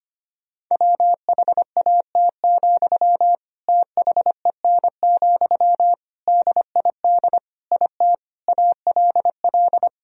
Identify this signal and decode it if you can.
Morse code, heard as a keyed tone, about 25 wpm, 700 Hz. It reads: WHAT, THEN, DID IT ALL
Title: Sentences from Top 1000 Words - Review Part 4 - 25wpm